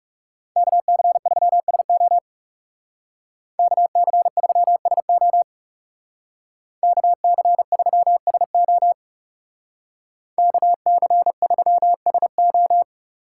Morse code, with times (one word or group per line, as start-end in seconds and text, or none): 0.6-2.2 KC3SO
3.6-5.4 KC3SO
6.8-8.9 KC3SO
10.4-12.8 KC3SO